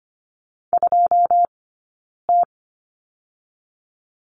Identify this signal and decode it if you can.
Morse code, heard as a keyed tone, about 25 wpm, 700 Hz. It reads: 2 T
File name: Lesson 15d - Number 5 - Two Characters 25wpm